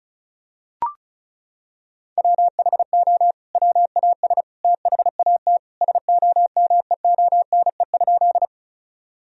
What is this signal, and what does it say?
Morse code, heard as a keyed tone, about 34 wpm, 700 Hz. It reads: WHO WAS THAT SOMEONE?
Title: Sentences from Top 1000 Words - Review Part 4 - 35wpm